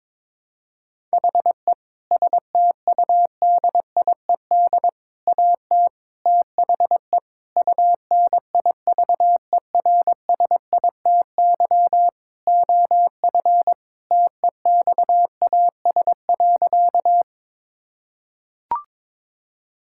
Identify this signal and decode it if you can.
Morse code, heard as a keyed tone, about 22 wpm, 700 Hz. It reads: HE STUDIED AT THE UNIVERSITY OF TEXAS.
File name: practice / 22wpm / Sentences with 4-Word Phrases - 22wpm